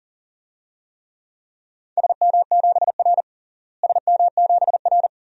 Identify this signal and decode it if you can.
Morse code, heard as a keyed tone, about 40 wpm, 700 Hz. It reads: SM7R SM7R
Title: DXCC-100 Call Signs - 4 Characters - Repeated 2x - 40wpm